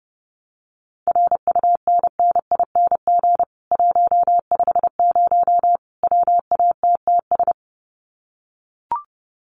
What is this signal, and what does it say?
Morse code, heard as a keyed tone, about 30 wpm, 700 Hz. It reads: RUNNING 150 WATTS
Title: Common QSO Phrases 30wpm